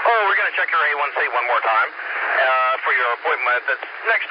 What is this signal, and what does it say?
Emergency Alert System (SAME) tones, heard as harsh buzzing bursts, with speech. It is heard from 0.3 to 3.8 s.